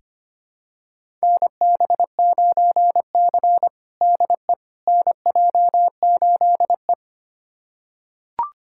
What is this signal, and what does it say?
Morse code, heard as a keyed tone, about 25 wpm, 700 Hz. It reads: NB9C DE NJ8E
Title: USA Callsign DE Callsign - 4 Characters - 25wpm